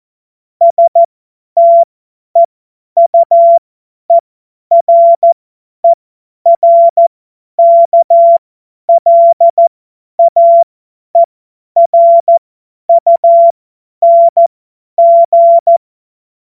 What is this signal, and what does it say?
Morse code, heard as a keyed tone, about 14 wpm, spaced as for 10 wpm, 680 Hz. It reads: STEUERERKLAERUNG
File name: Witz-02-September-2025-50-F70